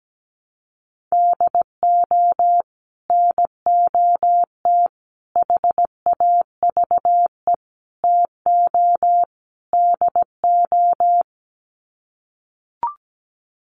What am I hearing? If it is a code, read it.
Morse code, heard as a keyed tone, about 17 wpm, 700 Hz. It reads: DO NOT HAVE TO DO
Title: Sets of 5 Words 17wpm